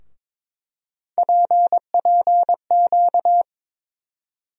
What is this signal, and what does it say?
Morse code, heard as a keyed tone, about 21 wpm, 700 Hz. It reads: PPQ
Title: USA Call Sign Postfixes - 3 Letters 22wpm